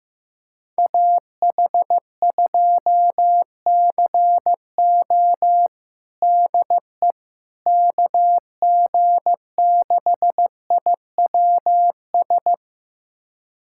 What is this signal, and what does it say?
Morse code, heard as a keyed tone, about 15 wpm, 700 Hz. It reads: AH2CO DE KG6IWS
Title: USA Callsign DE Callsign 15wpm